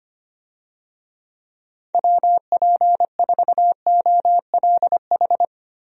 Morse code, heard as a keyed tone, about 25 wpm, 700 Hz. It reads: WP4OLH
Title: USA Call Signs - 6 Characters - Repeated 3x - 25wpm